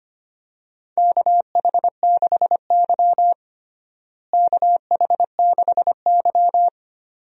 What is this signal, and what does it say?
Morse code, heard as a keyed tone, about 25 wpm, 700 Hz. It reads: KH6Y KH6Y
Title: DXCC-100 Call Signs - 4 Characters - Repeated 2x - 25wpm